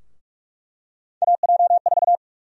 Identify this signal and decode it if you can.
Morse code, heard as a keyed tone, about 46 wpm, 700 Hz. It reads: AJ4
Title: USA Call Sign Prefixes - 45wpm